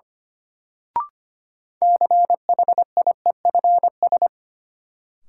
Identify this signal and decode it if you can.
Morse code, heard as a keyed tone, about 25 wpm, 700 Hz. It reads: CHIEFS
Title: Sets of 5 Words Encore - New Word Drill - 25wpm